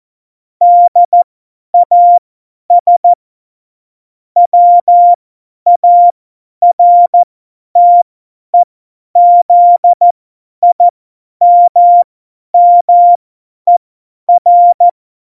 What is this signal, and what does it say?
Morse code, heard as a keyed tone, about 14 wpm, spaced as for 10 wpm, 700 Hz. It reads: DAS WARTEZIMMER
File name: Morsekurs / Witze / Witz 08. September 2020, 50 F70